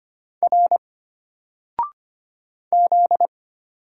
Morse code, heard as a keyed tone, about 25 wpm, 700 Hz. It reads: R Z
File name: Single-Letter-Number-25wpm